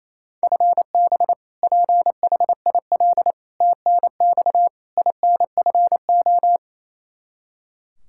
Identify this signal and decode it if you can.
Morse code, heard as a keyed tone, about 27 wpm, 700 Hz. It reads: FB PHIL TNX INFO
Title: Practice QSO - Exchange 3 - More Info - 28wpm